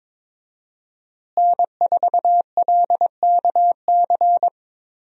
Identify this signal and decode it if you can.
Morse code, heard as a keyed tone, about 22 wpm, 700 Hz. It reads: N4LKC
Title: USA Call Signs - 5 Characters - Repeated 2x - 22wpm